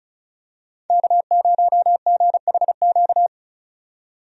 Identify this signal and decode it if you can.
Morse code, heard as a keyed tone, about 36 wpm, 700 Hz. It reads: K0GHQ
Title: USA Call Signs - 5 Characters - 35wpm